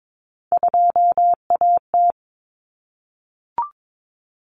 Morse code, heard as a keyed tone, about 22 wpm, 700 Hz. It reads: 2AT
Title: Common Cut Numbers - 3-Digit 22wpm